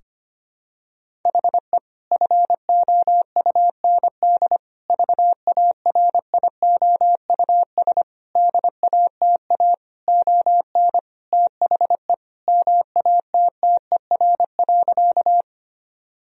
Morse code, heard as a keyed tone, about 25 wpm, 700 Hz. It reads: HE FOUND VARIOUS DATA ON THE MATTER.
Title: Sentences from Top 1000 Words - Review Part 4 - 25wpm